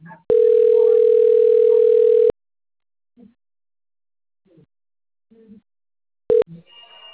Ringback tone (North America), 440 + 480 Hz.